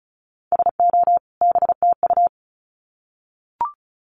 Morse code, heard as a keyed tone, about 34 wpm, 700 Hz. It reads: SO BTU
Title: Common QSO Phrases 35wpm